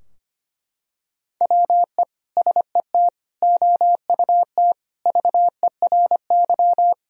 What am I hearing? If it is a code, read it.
Morse code, heard as a keyed tone, about 25 wpm, 700 Hz. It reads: WE SET OUT VERY
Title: Sentences from Top 1000 Words - Review Part 2 - 25wpm